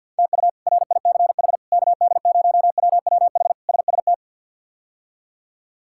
Morse code, heard as a keyed tone, about 52 wpm, 700 Hz. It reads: TU RICH KD0WWH SST
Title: K1USN SST QSO Practice - 50wpm